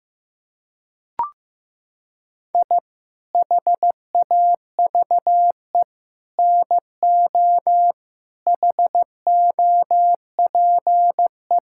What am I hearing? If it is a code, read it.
Morse code, heard as a keyed tone, about 15 wpm, 700 Hz. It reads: I HAVE NO HOPE
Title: Sentences from Top 1000 Words - Review Part 5 - 15wpm